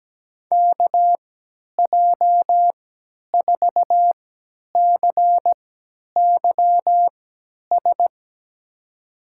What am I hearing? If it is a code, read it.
Morse code, heard as a keyed tone, about 17 wpm, spaced as for 10 wpm, 700 Hz. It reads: KJ4CYS
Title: USA Call Signs - 3x Character Spacing - 17wpm